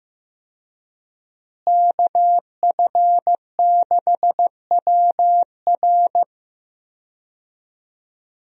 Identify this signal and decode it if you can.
Morse code, heard as a keyed tone, about 15 wpm, 700 Hz. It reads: KF6WR
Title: USA Call Signs - 5 Characters - Repeated 2x - 15wpm